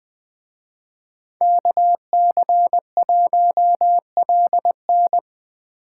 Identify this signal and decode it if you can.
Morse code, heard as a keyed tone, about 20 wpm, 700 Hz. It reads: KC1LN
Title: Speed Racing - USA 2x2 Call Signs 17wpm